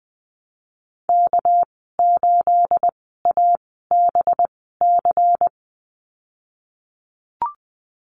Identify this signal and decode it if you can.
Morse code, heard as a keyed tone, about 20 wpm, spaced as for 15 wpm, 700 Hz. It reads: K8ABC
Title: 238 Lesson 38g - Letter q - Call Signs 20wpm